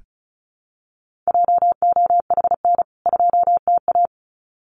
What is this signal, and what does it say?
Morse code, heard as a keyed tone, about 34 wpm, 700 Hz. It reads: JOHN 2TA
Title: CWT Exchange 35wpm